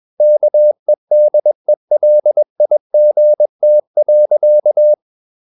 Morse code, heard as a keyed tone, about 21 wpm, 600 Hz. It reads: KEDELIGT.
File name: Ferie QRS 21wpm